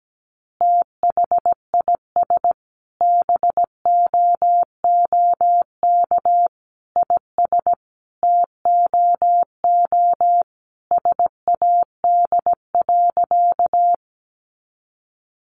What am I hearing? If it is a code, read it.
Morse code, heard as a keyed tone, about 17 wpm, 700 Hz. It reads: THIS BOOK IS TOO SAD.
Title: Additional Easy Phrases 17wpm